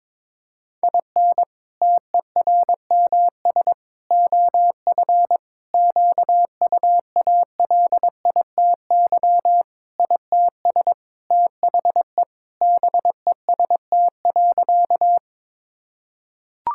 Morse code, heard as a keyed tone, about 22 wpm, 700 Hz. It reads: IN TERMS OF QUALITY ITS THE BEST.
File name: practice / 22wpm / Sentences with 3-Word Phrases - 22wpm